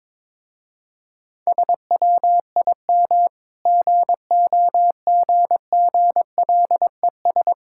Morse code, heard as a keyed tone, about 22 wpm, 690 Hz. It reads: SWIM GOGGLES
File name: Difficult Words with Hints - 22wpm